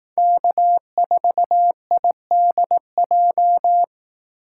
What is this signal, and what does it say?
Morse code, heard as a keyed tone, about 18 wpm, 700 Hz. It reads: K4IDJ
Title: Speed Racing - USA 1x3 Call Signs 15wpm